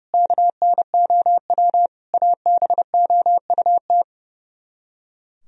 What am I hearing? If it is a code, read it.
Morse code, heard as a keyed tone, about 38 wpm, 700 Hz. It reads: KNOW ABOUT